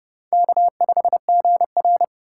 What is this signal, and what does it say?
Morse code, heard as a keyed tone, about 30 wpm, 700 Hz. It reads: K5GR
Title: USA Call Signs - 4 Characters - 4x Character Spacing - 30wpm